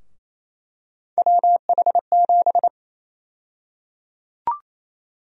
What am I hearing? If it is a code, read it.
Morse code, heard as a keyed tone, about 28 wpm, 700 Hz. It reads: WH7
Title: USA Call Sign Prefixes - Repeated 2x - 28wpm